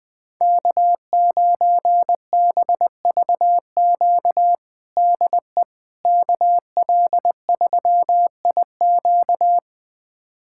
Morse code, heard as a keyed tone, about 20 wpm, 700 Hz. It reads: K9BVQ DE KL3IQ
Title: USA Callsign DE Callsign - 5 Characters - 20wpm